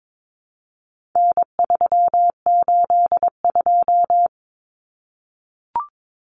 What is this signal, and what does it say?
Morse code, heard as a keyed tone, about 22 wpm, 700 Hz. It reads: N382
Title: Common Cut Numbers - 4-Digit 22wpm